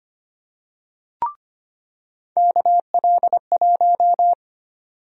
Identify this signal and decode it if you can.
Morse code, heard as a keyed tone, about 25 wpm, 700 Hz. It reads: KL1
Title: USA Call Sign Prefixes - 25wpm